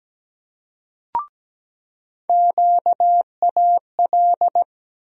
Morse code, heard as a keyed tone, about 17 wpm, 700 Hz. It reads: QAL
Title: USA Call Sign Postfixes - 3 Letters 17wpm